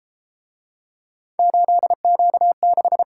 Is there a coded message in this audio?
Yes: Morse code from 1.4 to 3.0 s, heard as a keyed tone, about 32 wpm, 700 Hz. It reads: 8Q6